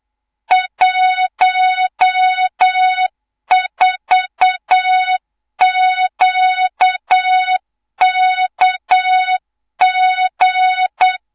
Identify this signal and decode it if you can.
Morse code, heard as a keyed tone, about 8 wpm, 3700 Hz, with noise. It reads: 14QKG
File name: Morze-HA2MN / hang / 24 GYAKORLAT